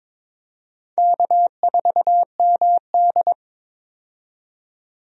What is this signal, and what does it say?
Morse code, heard as a keyed tone, about 22 wpm, 700 Hz. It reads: K4MD
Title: USA Call Signs - 4 Characters - 22wpm